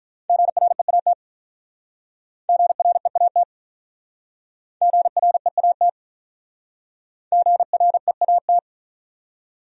Morse code, heard as a keyed tone, about 48 wpm, 700 Hz. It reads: GREAT GREAT GREAT GREAT